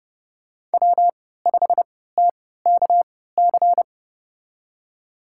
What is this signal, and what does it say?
Morse code, heard as a keyed tone, about 32 wpm, spaced as for 17 wpm, 700 Hz. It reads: W5TKC
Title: USA Call Signs - 5 Characters - 3x Character Spacing - 30wpm